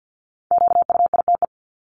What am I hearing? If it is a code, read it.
Morse code, heard as a keyed tone, about 50 wpm, 700 Hz. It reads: QUITE
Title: Top 501-750 Words 50wpm